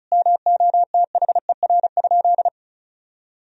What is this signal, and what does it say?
Morse code, heard as a keyed tone, about 34 wpm, 700 Hz. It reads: MOTHER?
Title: Sentences from Top 1000 Words - Review Part 2 - 35wpm